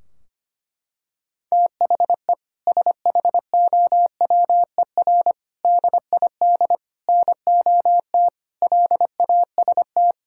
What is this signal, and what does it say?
Morse code, heard as a keyed tone, about 25 wpm, 700 Hz. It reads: THE SHOWER DID NOT LAST
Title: Sentences from Top 1600 Words 25wpm